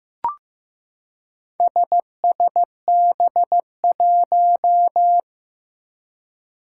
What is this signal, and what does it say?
Morse code, heard as a keyed tone, about 15 wpm, 700 Hz. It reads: SSB1
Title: Four Challenging Letter-Number Combinations 15wpm